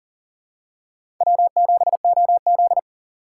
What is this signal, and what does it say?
Morse code, heard as a keyed tone, about 40 wpm, 700 Hz. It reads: W7OZ
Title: USA Call Signs - 4 Characters - 2x Character Spacing - 40wpm